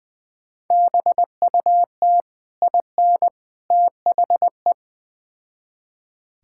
Morse code, heard as a keyed tone, about 20 wpm, 700 Hz. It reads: BUT IN THE
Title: Build A Phrase - 3-Word Phrases - 20wpm